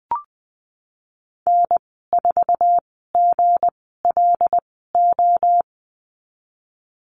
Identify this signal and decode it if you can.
Morse code, heard as a keyed tone, about 20 wpm, spaced as for 15 wpm, 700 Hz. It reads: N4GLO
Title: 173 Lesson 28g - Number 9 - Call Signs 20wpm